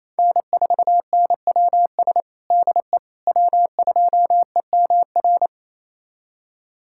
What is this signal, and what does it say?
Morse code, heard as a keyed tone, about 28 wpm, 700 Hz. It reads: N4NWS DE W2EMR
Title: USA Callsign DE Callsign - 5 Characters - 28wpm